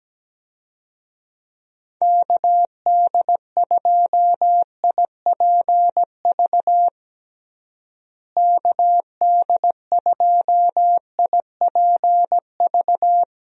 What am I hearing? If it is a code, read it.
Morse code, heard as a keyed tone, about 17 wpm, 700 Hz. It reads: KD2IPV KD2IPV
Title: USA Call Signs - 6 Characters - Repeated 3x - 17wpm